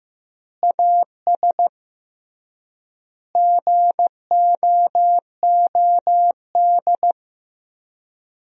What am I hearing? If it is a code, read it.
Morse code, heard as a keyed tone, about 15 wpm, 700 Hz. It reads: AS GOOD
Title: 3x Word-Spacing - Sentences from Top 200 Words - 15wpm